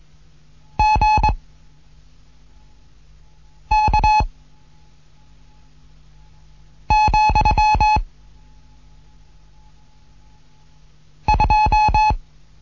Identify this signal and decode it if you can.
Morse code, heard as a keyed tone, about 21 wpm, 850 Hz. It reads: G K , 2
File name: Morsen / Kurs A 15